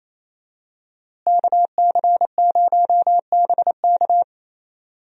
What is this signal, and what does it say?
Morse code, heard as a keyed tone, about 28 wpm, 700 Hz. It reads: KC0BK
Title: USA Call Signs - 5 Characters - Repeated 3x - 28wpm